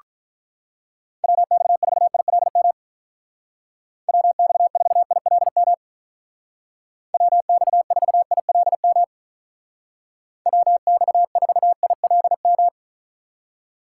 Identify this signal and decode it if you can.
Morse code, heard as a keyed tone, about 43 wpm, 700 Hz. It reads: WX4ILM WX4ILM WX4ILM WX4ILM
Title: Speed Racing - USA Call Signs 35wpm